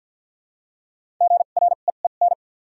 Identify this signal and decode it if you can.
Morse code, heard as a keyed tone, about 53 wpm, 700 Hz. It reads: G R E E N
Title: 2x Character Spacing - Top 500 Words - 50wpm